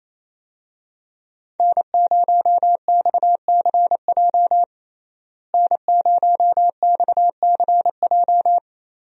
Morse code, heard as a keyed tone, about 28 wpm, 700 Hz. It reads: N0XCJ N0XCJ
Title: USA Call Signs - 5 Characters - Repeated 3x - 28wpm